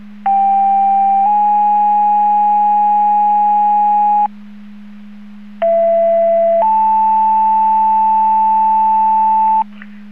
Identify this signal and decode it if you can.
Two-tone page, heard as a one-second tone, then a three-second tone, about 790 then 830 Hz, about 670 then 880 Hz. It is heard from 0.3 to 9.6 s.